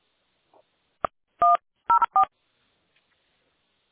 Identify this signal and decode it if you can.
DTMF, heard as telephone touch tones, tones of about 100 ms. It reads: 104